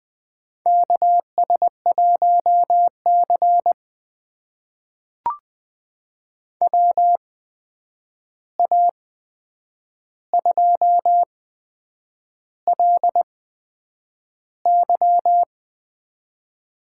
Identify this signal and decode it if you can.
Morse code, heard as a keyed tone, about 20 wpm, spaced as for 8 wpm, 700 Hz. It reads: KS1C WA2LY
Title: USA Call Signs - 8x Character Spacing - 20wpm